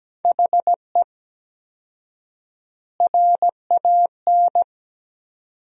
Morse code, heard as a keyed tone, about 17 wpm, 700 Hz. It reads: HE RAN
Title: 4x Word-Spacing - Sentences from Top 500 Words - 17wpm